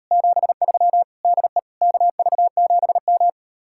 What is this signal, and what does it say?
Morse code, heard as a keyed tone, about 38 wpm, 700 Hz. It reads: 73 DE KV7M